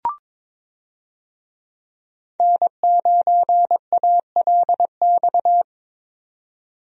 Morse code, heard as a keyed tone, about 22 wpm, 700 Hz. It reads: N9ALX
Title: USA Call Signs - 5 Characters - 22wpm